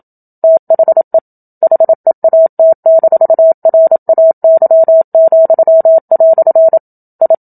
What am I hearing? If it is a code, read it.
Morse code, heard as a keyed tone, about 27 wpm, 650 Hz. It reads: THE HEAT-RAY," I
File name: Book Two / 01-Chapter 1 - 27 WPM